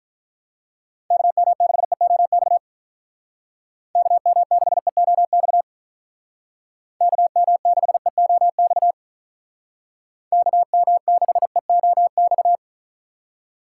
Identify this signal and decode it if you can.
Morse code, heard as a keyed tone, about 46 wpm, 700 Hz. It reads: KM6EOX KM6EOX KM6EOX KM6EOX